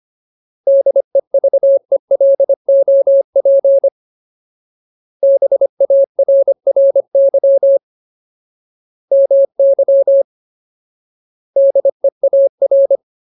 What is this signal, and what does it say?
Morse code, heard as a keyed tone, about 25 wpm, 550 Hz. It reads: DEVELOP BARRY MY DEAR
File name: Ultimate Word List / 25WPM Ultimate